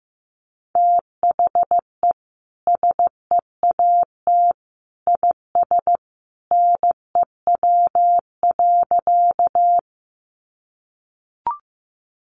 Morse code, heard as a keyed tone, about 15 wpm, 700 Hz. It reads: THE SEAT IS NEW.
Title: Sentences from Top 1500 Words 15wpm